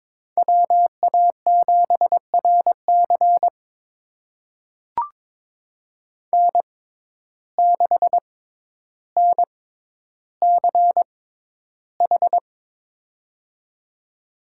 Morse code, heard as a keyed tone, about 22 wpm, spaced as for 11 wpm, 700 Hz. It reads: WA7RC N6NCH
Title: USA Call Signs - 5 Characters - 6x Character Spacing - 22wpm